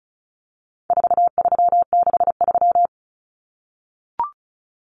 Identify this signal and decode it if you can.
Morse code, heard as a keyed tone, about 34 wpm, 700 Hz. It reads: 4363
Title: Common Cut Numbers - 4-Digit 35wpm